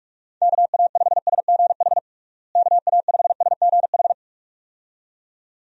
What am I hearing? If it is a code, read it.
Morse code, heard as a keyed tone, about 46 wpm, 700 Hz. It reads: KA5SGH KA5SGH
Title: USA Call Signs - 6 Characters - Repeated 2x - 45wpm